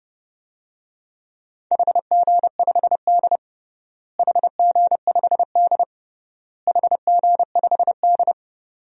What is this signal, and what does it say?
Morse code, heard as a keyed tone, about 32 wpm, 700 Hz. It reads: HG5D HG5D HG5D
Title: DXCC-100 Call Signs - 4 Characters - Repeated 3x - 30wpm